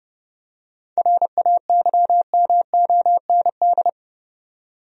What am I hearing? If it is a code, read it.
Morse code, heard as a keyed tone, about 32 wpm, 700 Hz. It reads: RAYMOND
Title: USA Common Names - 30wpm